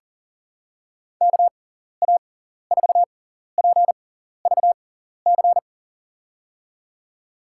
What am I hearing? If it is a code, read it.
Morse code, heard as a keyed tone, about 38 wpm, spaced as for 14 wpm, 700 Hz. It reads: KA4PVC
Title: USA Call Signs - 6 Characters - 6x Character Spacing - 40wpm